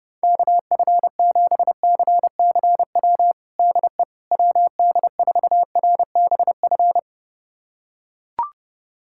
Morse code, heard as a keyed tone, about 30 wpm, 700 Hz. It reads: KF7CCW DE WD4RBF
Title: USA Callsign DE Callsign - 6 Characters - 30wpm